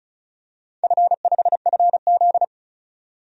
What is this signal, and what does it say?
Morse code, heard as a keyed tone, about 34 wpm, 700 Hz. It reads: F5FZ
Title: DXCC-100 Call Signs - 4 Characters - 35wpm